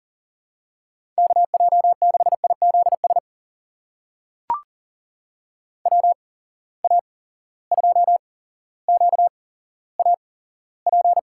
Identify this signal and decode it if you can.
Morse code, heard as a keyed tone, about 39 wpm, spaced as for 17 wpm, 700 Hz. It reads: KJ6IZS WA2QAP
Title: USA Call Signs - 6 Characters - 8x Character Spacing - 40wpm